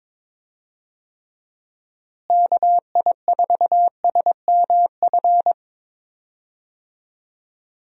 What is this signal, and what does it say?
Morse code, heard as a keyed tone, about 22 wpm, 700 Hz. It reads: KI4SMF
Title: USA Call Signs - 6 Characters - Repeated 3x - 22wpm